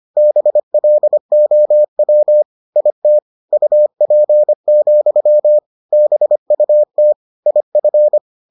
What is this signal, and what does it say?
Morse code, heard as a keyed tone, about 25 wpm, 600 Hz. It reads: BLOW IT UP, BUT IF